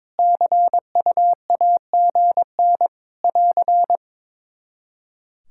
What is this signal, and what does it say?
Morse code, heard as a keyed tone, about 23 wpm, 700 Hz. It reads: CUAGN <AR>
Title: Practice QSO - Exchange 4 - Ending - 22wpm